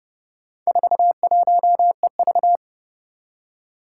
Morse code, heard as a keyed tone, about 30 wpm, 700 Hz. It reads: 41EV
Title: Four Challenging Letter-Number Combinations 30wpm